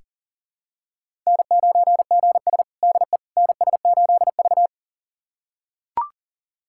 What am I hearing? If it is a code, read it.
Morse code, heard as a keyed tone, about 40 wpm, 700 Hz. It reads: N9GS DE NS8V
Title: USA Callsign DE Callsign - 4 Characters - 40wpm